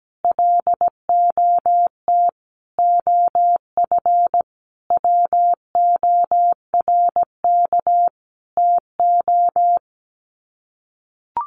Morse code, heard as a keyed tone, about 17 wpm, 700 Hz. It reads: LOT OF WORK TO
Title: Sets of 5 Words Encore2 17wpm